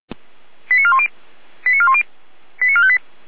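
Five-tone selective call (ZVEI), heard as quick runs of five tones, tones of about 70 ms. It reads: 89410 89410 89458
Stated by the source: ZVEI five-tone selective call on 455, received by SDR.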